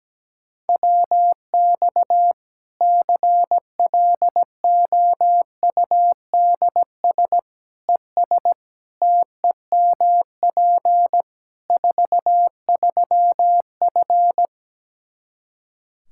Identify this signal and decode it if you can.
Morse code, heard as a keyed tone, about 17 wpm, 700 Hz. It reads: WX CLOUDS ES TEMP 43F
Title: Practice QSO - Exchange 2 - Info - 17wpm